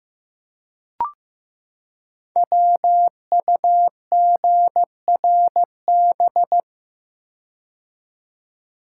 Morse code, heard as a keyed tone, about 15 wpm, 700 Hz. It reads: WUGRB
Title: Code Groups - 5 Letters - 15wpm